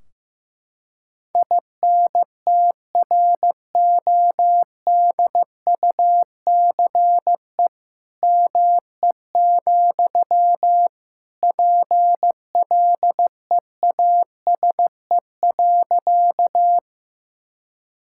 Morse code, heard as a keyed tone, about 15 wpm, 700 Hz. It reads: INTRODUCE ME, PLEASE.
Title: Sentences from Top 1600 Words 15wpm